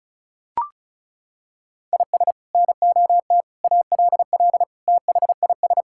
Morse code, heard as a keyed tone, about 36 wpm, 700 Hz. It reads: IS NOT ALL THIS
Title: Sentences from Top 1600 Words 35wpm